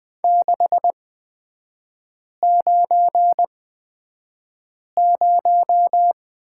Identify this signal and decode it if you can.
Morse code, heard as a keyed tone, about 20 wpm, spaced as for 5 wpm, 700 Hz. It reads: 690
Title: 243 Lesson 39e - Letter z - Three Characters 20wpm